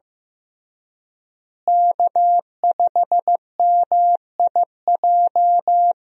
Morse code, heard as a keyed tone, about 16 wpm, 700 Hz. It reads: K5MIJ
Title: USA Call Signs - 5 Characters - Repeated 3x - 15wpm